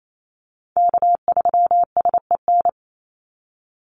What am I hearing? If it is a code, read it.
Morse code, heard as a keyed tone, about 28 wpm, 700 Hz. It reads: K3SEN
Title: Call Signs - 28wpm